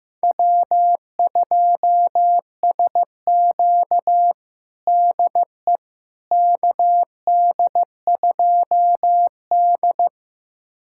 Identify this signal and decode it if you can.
Morse code, heard as a keyed tone, about 16 wpm, 700 Hz. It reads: W2SQ DE KD2D
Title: USA Callsign DE Callsign - 4 Characters - 15wpm